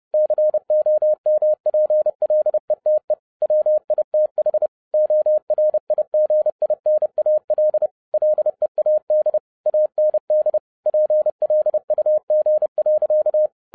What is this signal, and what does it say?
Morse code, heard as a keyed tone, about 30 wpm, 610 Hz. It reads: COMPLETE WITH ORIGINAL LEAD AND PLUG.
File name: 2011-10-mp3 / n10-Czech Army Morse Keys 30wpm